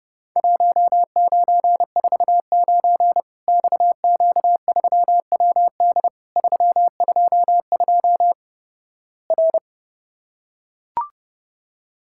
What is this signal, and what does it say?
Morse code, heard as a keyed tone, about 30 wpm, 700 Hz. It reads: 1949 XQ3WD 322 R
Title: Worked All Europe DX Contest - QTC Exchange - 30wpm